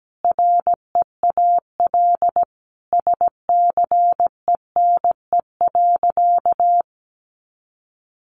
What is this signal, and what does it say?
Morse code, heard as a keyed tone, about 17 wpm, 700 Hz. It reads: REAL SCENE.
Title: Sentences from Top 1100 Words 17wpm